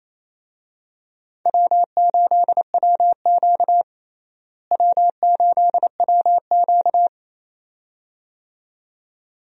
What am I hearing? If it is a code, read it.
Morse code, heard as a keyed tone, about 28 wpm, 700 Hz. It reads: W8WQ W8WQ
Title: USA Call Signs - 4 Characters - Repeated 2x - 28wpm